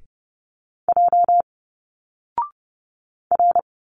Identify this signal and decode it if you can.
Morse code, heard as a keyed tone, about 30 wpm, 700 Hz. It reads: J R
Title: Single Letters 30wpm